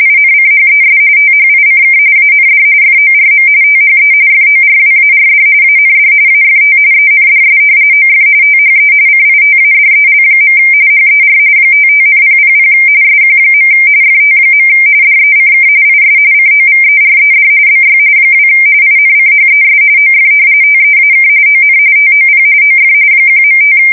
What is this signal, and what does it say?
RTTY, about 45.45 baud. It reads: INCHES THICK AND FORM THE FORWARD AND AFT ENDS OF THE CITADEL. HER 55,000 SQ. FT. DECK IS COVERED BY 2 INCHES OF TEAK ON TOP OF 1.5
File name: fsk170 / battleshipnj